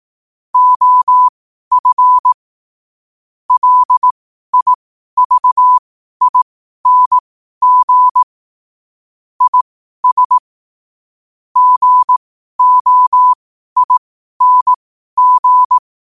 Morse code, heard as a keyed tone, about 18 wpm, spaced as for 12 wpm, 1000 Hz. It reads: OF LIVING IS GOING
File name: qotdfiles / 12wpm